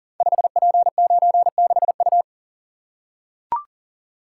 Morse code, heard as a keyed tone, about 38 wpm, 690 Hz. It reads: 5P96U
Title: Code Groups - 5 Letter:Numbers - 40wpm